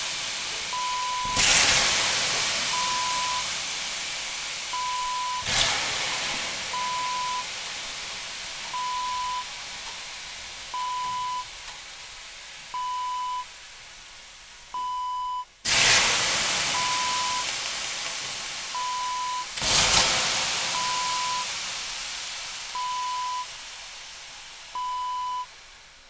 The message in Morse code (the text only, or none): none